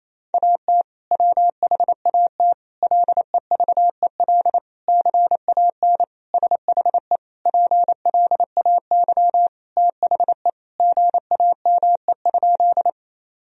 Morse code, heard as a keyed tone, about 28 wpm, 700 Hz. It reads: AT WHAT LEVEL CAN SHE PLAY THE GAME?
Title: Sentences from Top 1000 Words - Review Part 4 - 28wpm